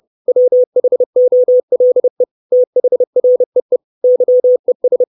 Morse code, heard as a keyed tone, about 30 wpm, 500 Hz. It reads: WHOLE THREE YES
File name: 632 Words 30WPM